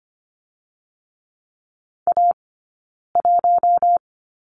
Morse code, heard as a keyed tone, about 25 wpm, 700 Hz. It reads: A 1
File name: course / 25:10wpm / 092 Lesson 16d - Letter c - Two Characters 25wpm